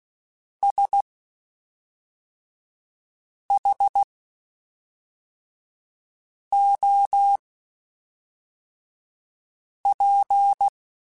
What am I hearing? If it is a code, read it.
Morse code, heard as a keyed tone, about 16 wpm, spaced as for 3 wpm, 800 Hz. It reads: SHOP
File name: Spaced Characters / Spaced Characters-mp3s / Weather4 24 sp @5